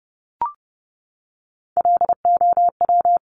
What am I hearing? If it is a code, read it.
Morse code, heard as a keyed tone, about 30 wpm, 700 Hz. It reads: LOW
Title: Top Three Letter Combinations in Words 30wpm